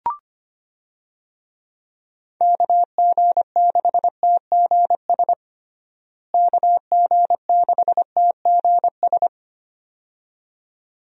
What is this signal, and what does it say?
Morse code, heard as a keyed tone, about 25 wpm, 700 Hz. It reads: KG6TGS KG6TGS
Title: USA Call Signs - 6 Characters - Repeated 2x - 25wpm